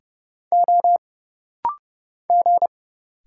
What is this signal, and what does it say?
Morse code, heard as a keyed tone, about 30 wpm, 700 Hz. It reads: O G
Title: Single Letters - Rapid-Fire 30wpm